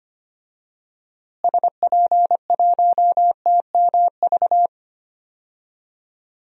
Morse code, heard as a keyed tone, about 25 wpm, 700 Hz. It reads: SP1TMV
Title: DXCC-100 Call Signs - 6 Characters - Repeated 2x - 25wpm